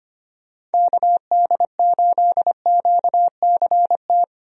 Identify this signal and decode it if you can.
Morse code, heard as a keyed tone, about 25 wpm, 700 Hz. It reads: KD8QCT
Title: USA Call Signs - 6 Characters - 25wpm